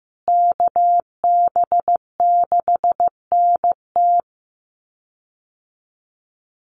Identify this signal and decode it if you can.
Morse code, heard as a keyed tone, about 15 wpm, 700 Hz. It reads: KB6NT
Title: USA Call Signs Repeated 3x 15wpm